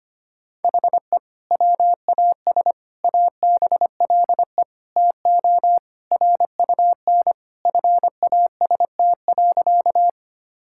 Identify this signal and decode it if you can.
Morse code, heard as a keyed tone, about 25 wpm, 700 Hz. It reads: HE WAS ABLE TO RUN FAST.
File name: practice / 25wpm / Sentences with 2-Word Phrases - 25wpm